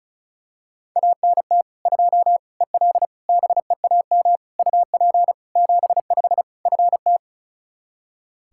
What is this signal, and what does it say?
Morse code, heard as a keyed tone, about 36 wpm, 700 Hz. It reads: ANT 2 EL BEAM UP 75 FT
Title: Practice QSO - Exchange 2 - Info - 35wpm